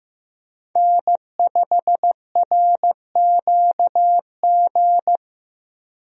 Morse code, heard as a keyed tone, about 15 wpm, 700 Hz. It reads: N5RQG
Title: USA Call Signs - 5 Characters - Repeated 3x - 15wpm